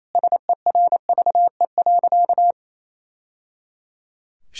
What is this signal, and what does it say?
Morse code, heard as a keyed tone, about 28 wpm, 700 Hz. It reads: SERVE.
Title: Sentences from Top 1000 Words - Review Part 2 - 28wpm